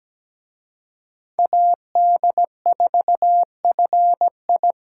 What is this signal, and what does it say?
Morse code, heard as a keyed tone, about 20 wpm, 700 Hz. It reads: AD4FI
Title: Speed Racing - USA Call Signs 17wpm